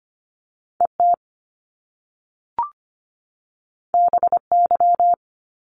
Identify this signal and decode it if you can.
Morse code, heard as a keyed tone, about 25 wpm, 700 Hz. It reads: E T B Y
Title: Top Two Letter Combinations in Words 25wpm